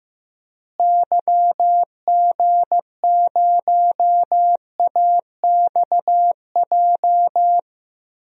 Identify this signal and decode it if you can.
Morse code, heard as a keyed tone, about 15 wpm, 700 Hz. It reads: YG0AXJ
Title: DXCC-100 Call Signs - 6 Characters - Repeated 3x - 15wpm